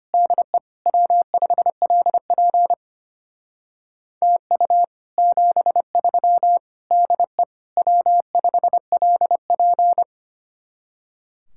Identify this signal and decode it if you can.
Morse code, heard as a keyed tone, about 32 wpm, 700 Hz. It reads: DE W5LP TU 73 DE W5LP